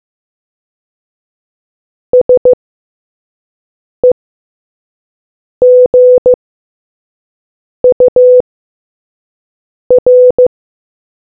Morse code, heard as a keyed tone, about 15 wpm, 500 Hz. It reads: S E G U R